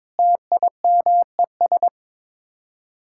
Morse code, heard as a keyed tone, about 22 wpm, 700 Hz. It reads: TIMES
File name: practice / 22wpm / Sets of 7 Words - 22wpm